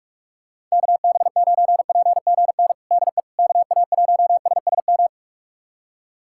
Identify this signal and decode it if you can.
Morse code, heard as a keyed tone, about 41 wpm, 700 Hz. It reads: KB9PGN DE KA1SSM
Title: USA Callsign DE Callsign - 6 Characters - 45wpm